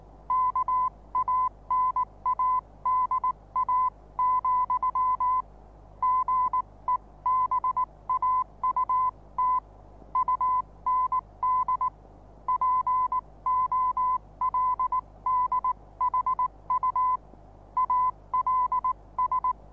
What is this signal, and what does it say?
Morse code, heard as a keyed tone, about 19 wpm, spaced as for 17 wpm, 1010 Hz. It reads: KANADA, GEBAUT UND POLDHU ALS
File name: repos / Morsekurs / DJ4UF Morsen26